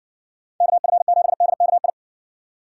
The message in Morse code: KF7DKI

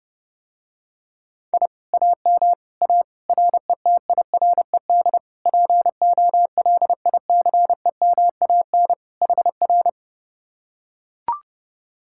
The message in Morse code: I AM A RETIRED POLICEMAN HR

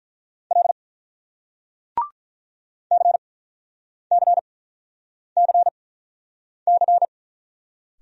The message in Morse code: L CCCC

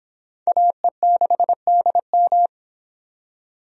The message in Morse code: AE6DM

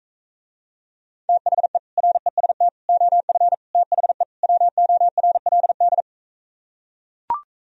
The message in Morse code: THE REST OF THE WORLD

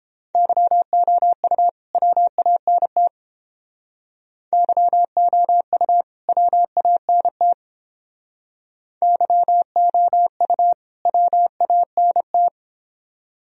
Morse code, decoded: YOU WANT YOU WANT YOU WANT

